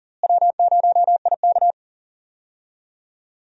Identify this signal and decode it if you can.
Morse code, heard as a keyed tone, about 40 wpm, 700 Hz. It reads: W0IK
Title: USA Call Signs - 4 Characters - Repeated 3x - 40wpm